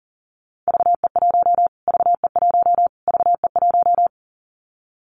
Morse code, heard as a keyed tone, about 40 wpm, 700 Hz. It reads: VE1 VE1 VE1
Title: Callsign Prefixes for Top 25 DXCC - Repeated 3x - 40wpm